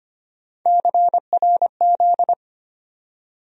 Morse code, heard as a keyed tone, about 25 wpm, 700 Hz. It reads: CRZ
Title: USA Call Sign Postfixes - 3 Letters 25wpm